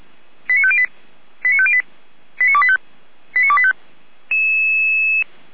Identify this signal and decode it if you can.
Five-tone selective call (ZVEI), heard as quick runs of five tones, tones of about 70 ms. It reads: 89508 89508 89285 89285
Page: SDR/ZVEI